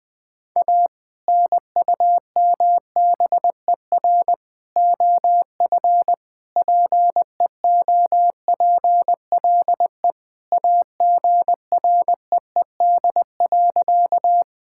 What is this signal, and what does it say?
Morse code, heard as a keyed tone, about 20 wpm, 700 Hz. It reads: A NUMBER OF PEOPLE AGREED.
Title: Sentences with 3-Word Phrases - 20wpm